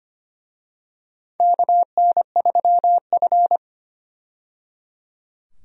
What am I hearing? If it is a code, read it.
Morse code, heard as a keyed tone, about 25 wpm, 700 Hz. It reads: KN3F
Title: USA Call Signs - 4 Characters - 25wpm